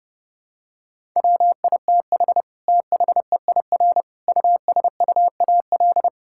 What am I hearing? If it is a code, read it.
Morse code, heard as a keyed tone, about 32 wpm, 700 Hz. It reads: WITH THEIR USUAL
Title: Sentences from Top 1000 Words - Review Part 6 - 30wpm